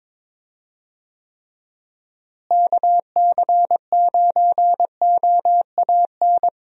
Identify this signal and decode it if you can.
Morse code, heard as a keyed tone, about 22 wpm, 700 Hz. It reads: KC9OAN
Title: USA Call Signs - 6 Characters - Repeated 3x - 22wpm